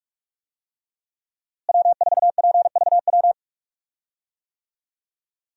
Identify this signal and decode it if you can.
Morse code, heard as a keyed tone, about 46 wpm, 700 Hz. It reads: W4PVW
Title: USA Call Signs - 5 Characters - Repeated 3x - 45wpm